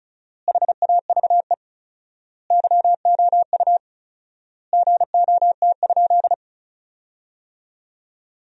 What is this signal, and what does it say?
Morse code, heard as a keyed tone, about 36 wpm, 700 Hz. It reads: HAVE YOU GOT?
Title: 4x Word-Spacing - Sentences from Top 300 Words - 35wpm